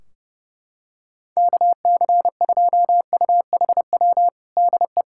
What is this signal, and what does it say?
Morse code, heard as a keyed tone, about 30 wpm, 700 Hz. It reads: KC2UHW DE
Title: Practice QSO - Exchange 4 - Ending - 30wpm